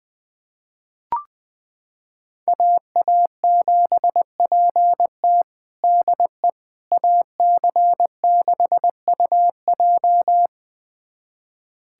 Morse code, heard as a keyed tone, about 20 wpm, 700 Hz. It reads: AA7PT DE AC6UJ
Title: USA Callsign DE Callsign - 5 Characters - 20wpm